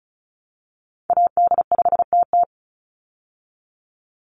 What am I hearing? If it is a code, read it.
Morse code, heard as a keyed tone, about 34 wpm, 700 Hz. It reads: AD5TT
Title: Call Signs Repeated 3x - 35wpm